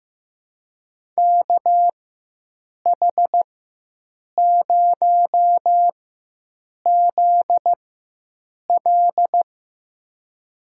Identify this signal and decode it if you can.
Morse code, heard as a keyed tone, about 15 wpm, spaced as for 7 wpm, 700 Hz. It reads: KH0ZL
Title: USA Call Signs - 5 Characters - 4x Character Spacing - 15wpm